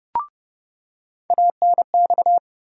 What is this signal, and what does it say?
Morse code, heard as a keyed tone, about 30 wpm, 700 Hz. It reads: ANX
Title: USA Call Sign Postfixes - 3 Letters 30wpm